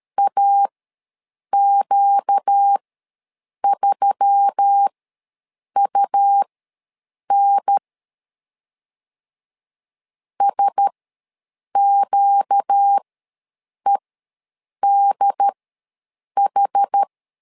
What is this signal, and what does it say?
Morse code, heard as a keyed tone, about 13 wpm, spaced as for 7 wpm, 800 Hz. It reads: AQ3UN SQEDH
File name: LEZIONI / 4 Q1S3 ws 12wpm 2038